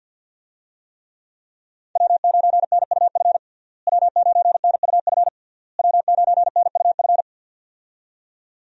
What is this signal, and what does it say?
Morse code, heard as a keyed tone, about 52 wpm, 700 Hz. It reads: W9NUF W9NUF W9NUF
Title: USA Call Signs - 5 Characters - Repeated 3x - 50wpm